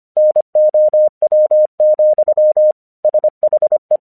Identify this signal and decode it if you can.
Morse code, heard as a keyed tone, about 26 wpm, 610 Hz. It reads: NOW, SHE
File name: 1 English 25wpm / AQC21